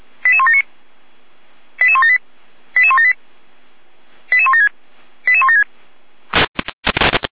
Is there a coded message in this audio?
Yes: five-tone selective call (ZVEI) from 0.3 to 5.6 s, heard as quick runs of five tones, tones of about 70 ms. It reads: 88179 88178 88178 88176 88176